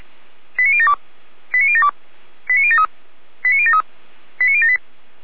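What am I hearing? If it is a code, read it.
Five-tone selective call (ZVEI), heard as quick runs of five tones, tones of about 70 ms. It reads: 89072 89072 89073 89073 89078
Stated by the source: Five-tone ZVEI selective call 89050 received on an SDR.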